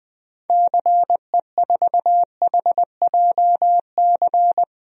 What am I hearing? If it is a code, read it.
Morse code, heard as a keyed tone, about 20 wpm, 700 Hz. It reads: CE4HJC
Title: DXCC-100 Call Signs - 6 Characters - Repeated 2x - 20wpm